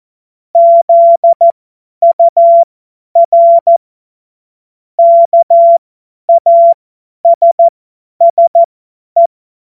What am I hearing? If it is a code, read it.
Morse code, heard as a keyed tone, about 14 wpm, spaced as for 10 wpm, 680 Hz. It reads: ZUR KASSE
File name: Morsekurs / Witze / Witz 20. Januar 2026 50 F70